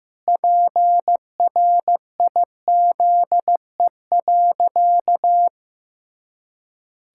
Morse code, heard as a keyed tone, about 15 wpm, 700 Hz. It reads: PRIZE.
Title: Pangram Challenge v2 - 15wpm